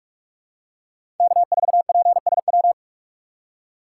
Morse code, heard as a keyed tone, about 45 wpm, 700 Hz. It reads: K4PSW
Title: USA Call Signs - 5 Characters - 4x Character Spacing - 45wpm